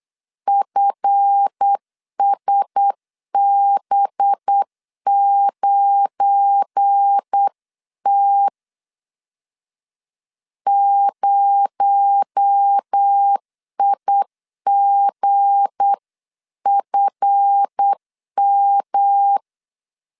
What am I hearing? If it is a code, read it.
Morse code, heard as a keyed tone, about 8 wpm, 800 Hz. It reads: FSB9T 0IGFM
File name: LEZIONI / 7 GF9Z / 7 GF9 8wpm 3545